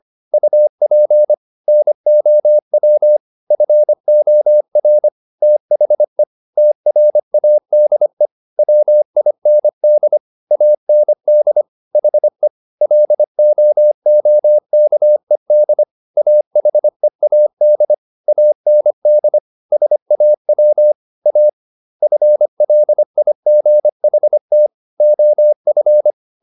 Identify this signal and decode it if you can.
Morse code, heard as a keyed tone, about 25 wpm, 600 Hz. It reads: UP NOW FOR THE TRADE WIND AND HE LOOKED AHEAD AND SAW A FLIGHT OF
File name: the old man and the sea part 1 25wpm